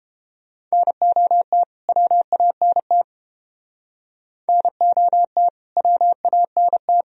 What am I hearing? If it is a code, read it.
Morse code, heard as a keyed tone, about 31 wpm, 700 Hz. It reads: NOT WANT NOT WANT